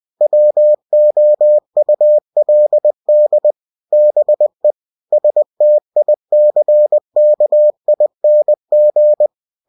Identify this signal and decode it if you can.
Morse code, heard as a keyed tone, about 20 wpm, 600 Hz. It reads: WOULD BE STICKING
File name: cw-books / winnie-the-pooh / winnie-20wpm-page-0085